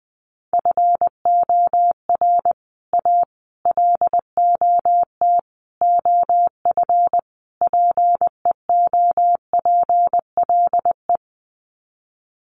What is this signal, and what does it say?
Morse code, heard as a keyed tone, about 20 wpm, 700 Hz. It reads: FOR A LOT OF PEOPLE
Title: Sets of 5 Words Encore 20wpm